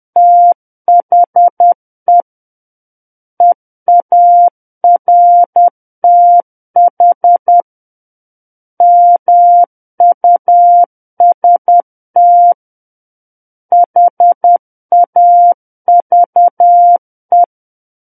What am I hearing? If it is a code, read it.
Morse code, heard as a keyed tone, about 10 wpm, 700 Hz. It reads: THE EARTH MUST HAVE
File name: Book One / 01-Chapter 1 - 10 WPM